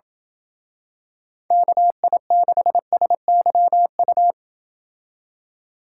Morse code, heard as a keyed tone, about 27 wpm, 700 Hz. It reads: KI6SYU